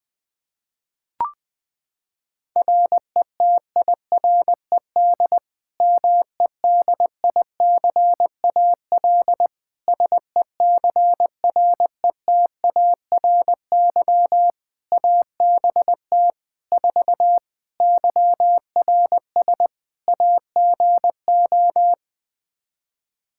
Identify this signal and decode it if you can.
Morse code, heard as a keyed tone, about 20 wpm, 700 Hz. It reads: RETIRED MEDICAL SECRETARY ABT 4 YRS AGO
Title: Practice QSO - Exchange 3 - More Info - 20wpm